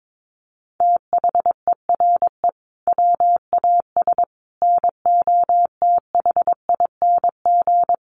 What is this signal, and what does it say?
Morse code, heard as a keyed tone, about 22 wpm, 700 Hz. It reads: THERE WAS NOTHING
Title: Sentences from Top 1300 Words 22wpm